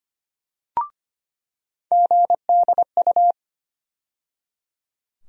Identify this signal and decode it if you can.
Morse code, heard as a keyed tone, about 25 wpm, 700 Hz. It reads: GDU
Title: USA Call Sign Postfixes - 3 Letters 25wpm